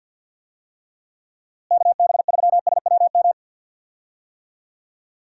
Morse code, heard as a keyed tone, about 51 wpm, 700 Hz. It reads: KB3SWM
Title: USA Call Signs - 6 Characters - Repeated 3x - 50wpm